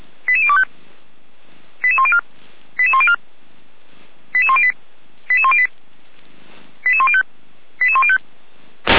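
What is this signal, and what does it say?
Five-tone selective call (ZVEI), heard as quick runs of five tones, tones of about 70 ms. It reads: 88025 88104 88104 88108 88108 88105 88105
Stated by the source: ZVEI five-tone selective call received by SDR.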